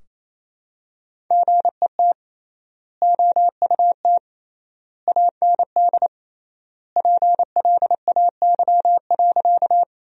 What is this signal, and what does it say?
Morse code, heard as a keyed tone, about 28 wpm, 700 Hz. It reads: GET OUT AND PLAY.
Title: 3x Word-Spacing - Sentences from Top 200 Words - 28wpm